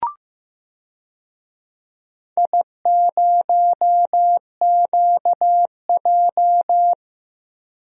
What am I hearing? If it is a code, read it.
Morse code, heard as a keyed tone, about 15 wpm, 700 Hz. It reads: I0QJ
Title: DXCC-100 Call Signs - 4 Characters - 15wpm